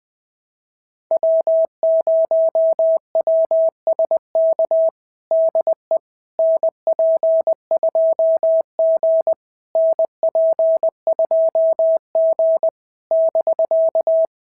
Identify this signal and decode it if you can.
Morse code, heard as a keyed tone, about 21 wpm, 690 Hz. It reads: W0WSK DE NP2G NP2G <BK>
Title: SOTA QSO Practice 20wpm